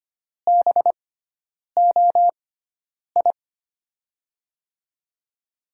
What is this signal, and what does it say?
Morse code, heard as a keyed tone, about 25 wpm, spaced as for 9 wpm, 700 Hz. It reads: BOI
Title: USA Call Signs - 6x Character Spacing - 25wpm